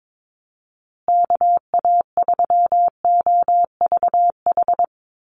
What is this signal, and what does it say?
Morse code, heard as a keyed tone, about 22 wpm, 700 Hz. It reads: KA3OVH
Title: USA Call Signs Repeated 2x 22wpm